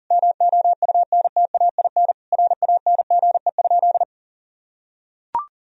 Morse code, heard as a keyed tone, about 40 wpm, 700 Hz. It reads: MOUNTAIN RANGE?
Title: Sentences from Top 1000 Words - Review Part 4 - 40wpm